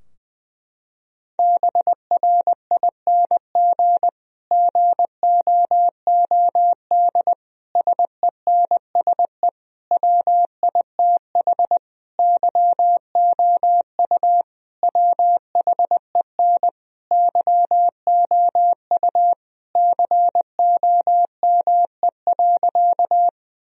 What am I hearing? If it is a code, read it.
Morse code, heard as a keyed tone, about 20 wpm, 700 Hz. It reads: BRING GOOD SENSE WITH YOU WHEN YOU COME.
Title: Sentences from Top 1000 Words - Review Part 4 - 20wpm